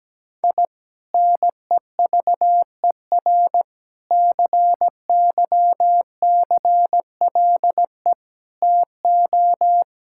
Morse code, heard as a keyed tone, about 17 wpm, 700 Hz. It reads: I NEVER CYCLE TO
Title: Sentences from Top 1000 Words - Review Part 6 - 17wpm